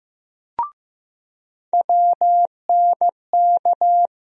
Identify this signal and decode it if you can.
Morse code, heard as a keyed tone, about 15 wpm, 700 Hz. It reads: WNK
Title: USA Call Sign Postfixes - 3 Letters 15wpm